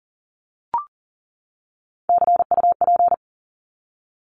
Morse code, heard as a keyed tone, about 40 wpm, 700 Hz. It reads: CUP
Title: Top 1501-1750 Words 40wpm